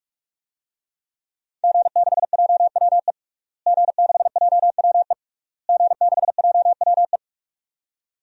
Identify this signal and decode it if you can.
Morse code, heard as a keyed tone, about 44 wpm, 700 Hz. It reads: G6JWE G6JWE G6JWE
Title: DXCC-100 Call Signs - 5 Characters - Repeated 3x - 45wpm